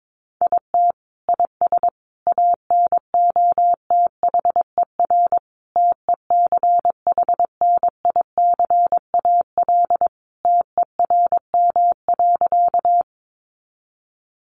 Morse code, heard as a keyed tone, about 22 wpm, 700 Hz. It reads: IT IS ANOTHER TECHNICAL TERM.
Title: Sentences from Top 1100 Words 22wpm